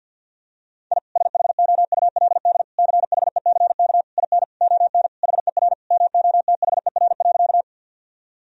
Morse code, heard as a keyed tone, about 52 wpm, 700 Hz. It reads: I SHOULD CHECK IN ON HER MOTHER.
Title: Sentences from Top 1000 Words - Review Part 3 - 50wpm